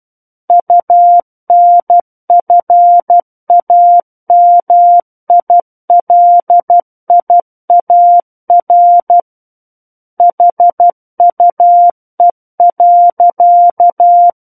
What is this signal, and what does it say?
Morse code, heard as a keyed tone, about 12 wpm, 700 Hz. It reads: UNFAMILIAR HUE.